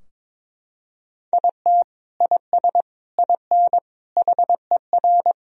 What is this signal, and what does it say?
Morse code, heard as a keyed tone, about 22 wpm, 700 Hz. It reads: IT IS IN HER
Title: Sentences from Top 1000 Words - Review Part 4 - 22wpm